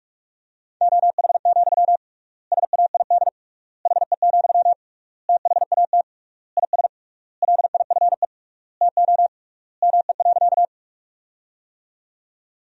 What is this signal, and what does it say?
Morse code, heard as a keyed tone, about 46 wpm, 700 Hz. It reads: OH, SAID HE, THAT IS LIFE TO ME.
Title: 3x Word-Spacing - Sentences from Top 500 Words - 45wpm